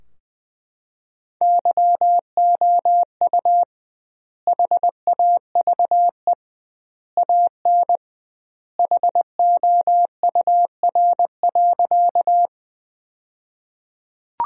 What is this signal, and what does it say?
Morse code, heard as a keyed tone, about 20 wpm, 700 Hz. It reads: YOU HAVE AN HOUR.
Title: 2x Word-Spacing - Sentences from Top 400 Words - 20wpm